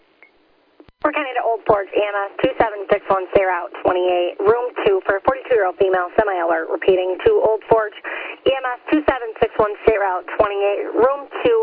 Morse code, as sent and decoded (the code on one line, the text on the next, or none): none